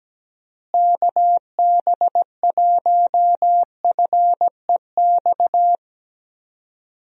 Morse code, heard as a keyed tone, about 17 wpm, 700 Hz. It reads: KB1FEX
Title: USA Call Signs - 6 Characters - Repeated 2x - 17wpm